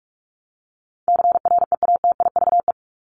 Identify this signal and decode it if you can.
Morse code, heard as a keyed tone, about 44 wpm, 700 Hz. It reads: CREATIVE
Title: Top 751-1000 Words 45wpm